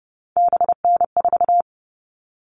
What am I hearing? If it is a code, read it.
Morse code, heard as a keyed tone, about 30 wpm, 700 Hz. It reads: BN4
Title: All Cut Numbers - 3-Digit 30wpm